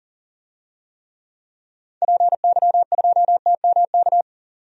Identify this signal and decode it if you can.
Morse code, heard as a keyed tone, about 40 wpm, 700 Hz. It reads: PY2TMK
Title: DXCC-100 Call Signs - 6 Characters - 40wpm